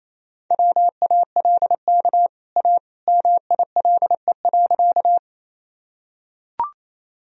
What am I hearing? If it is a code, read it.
Morse code, heard as a keyed tone, about 28 wpm, 700 Hz. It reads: WALK A MILE.
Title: Sentences from Top 1000 Words - Review Part 1 - 28wpm